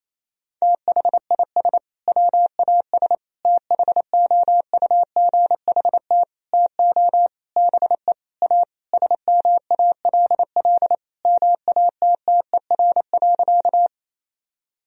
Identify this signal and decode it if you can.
Morse code, heard as a keyed tone, about 28 wpm, 700 Hz. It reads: THIS WAS THOUGHT TO BE A SMALL MATTER.
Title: Sentences from Top 1000 Words - Review Part 1 - 28wpm